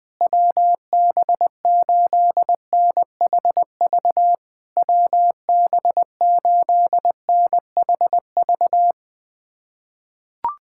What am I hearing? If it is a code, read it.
Morse code, heard as a keyed tone, about 20 wpm, 690 Hz. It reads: WB8NHV WB8NHV
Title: SOTA QSO Practice 20wpm